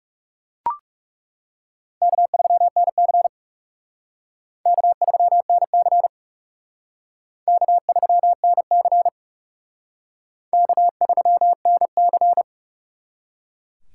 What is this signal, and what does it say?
Morse code, heard as a keyed tone, about 37 wpm, 700 Hz. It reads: K3NC K3NC K3NC K3NC
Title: Speed Racing - USA 1x2 Call Signs 30wpm